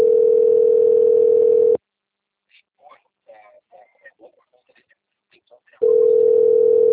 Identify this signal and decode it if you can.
Ringback tone (North America), 440 + 480 Hz, with speech.
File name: out- - - - - .29077